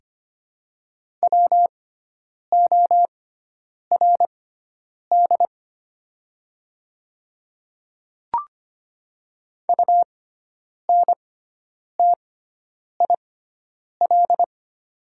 Morse code, heard as a keyed tone, about 25 wpm, spaced as for 9 wpm, 700 Hz. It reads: WORD UNTIL
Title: 6x Character Spacing - Top 300 Words - 25wpm